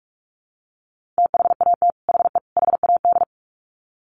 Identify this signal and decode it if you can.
Morse code, heard as a keyed tone, about 44 wpm, 700 Hz. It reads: THAT HE HAD
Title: Sets of 3 Words 45wpm